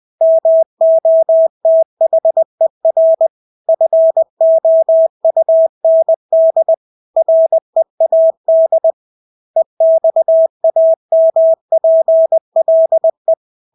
Morse code, heard as a keyed tone, about 20 wpm, 650 Hz. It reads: MOTHER FOUND READ EXAMPLE
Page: Engelske ord 20 wpm